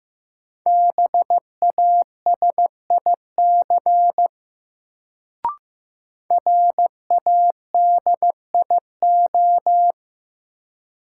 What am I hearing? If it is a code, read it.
Morse code, heard as a keyed tone, about 15 wpm, 700 Hz. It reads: BASIC RADIO
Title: Sentences from Top 1000 Words - Review Part 3 - New Word Drill - 15wpm